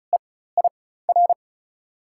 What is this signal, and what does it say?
Morse code, heard as a keyed tone, about 34 wpm, spaced as for 16 wpm, 690 Hz. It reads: EIR
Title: 4x Character Spacing - Top 100 Words - 35wpm